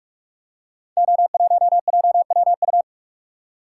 Morse code, heard as a keyed tone, about 44 wpm, 690 Hz. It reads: O1JWU
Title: Code Groups - 5 Letter:Numbers - 45wpm